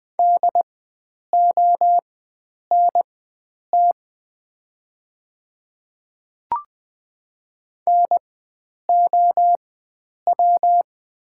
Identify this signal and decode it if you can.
Morse code, heard as a keyed tone, about 20 wpm, spaced as for 9 wpm, 700 Hz. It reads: DONT NOW
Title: 4x Character Spacing - Top 300 Words - 20wpm